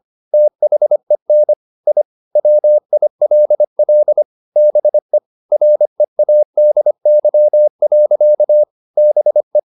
Morse code, heard as a keyed tone, about 25 wpm, 600 Hz. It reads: THEN I WILL BE READY. BE
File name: the old man and the sea part 1 25wpm